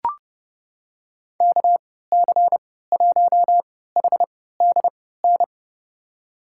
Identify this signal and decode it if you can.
Morse code, heard as a keyed tone, about 30 wpm, spaced as for 17 wpm, 700 Hz. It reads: KC1HDN
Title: USA Call Signs - 6 Characters - 3x Character Spacing - 30wpm